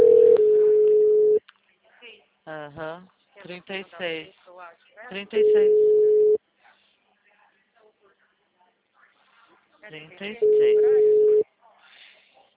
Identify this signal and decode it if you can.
Ringback tone (Europe, 425 Hz), with speech.
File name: out- -2011